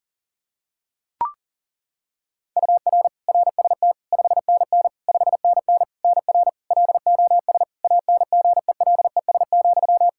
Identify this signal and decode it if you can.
Morse code, heard as a keyed tone, about 40 wpm, 700 Hz. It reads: UR RST 5NN 5NN NR LOS ANGELES,
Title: K1USN SST QSO Practice 40wpm